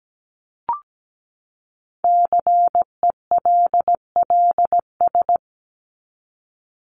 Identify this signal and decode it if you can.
Morse code, heard as a keyed tone, about 17 wpm, 700 Hz. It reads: CELLS
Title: ES:S Suffix - Top 100 Words with Base Words 17wpm